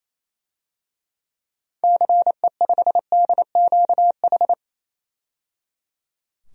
Morse code, heard as a keyed tone, about 28 wpm, 700 Hz. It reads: CE5DQH
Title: DXCC-100 Call Signs - 6 Characters - 28wpm